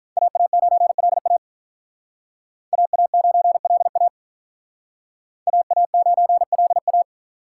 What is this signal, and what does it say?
Morse code, heard as a keyed tone, about 47 wpm, 700 Hz. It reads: AA9LA AA9LA AA9LA